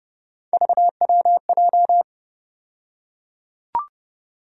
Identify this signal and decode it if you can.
Morse code, heard as a keyed tone, about 30 wpm, 700 Hz. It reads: VWJ
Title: USA Call Sign Postfixes - 3 Letters 30wpm